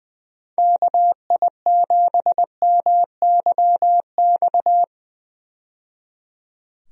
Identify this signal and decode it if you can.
Morse code, heard as a keyed tone, about 21 wpm, 700 Hz. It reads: KI7MYX
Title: USA Call Signs - 6 Characters - 20wpm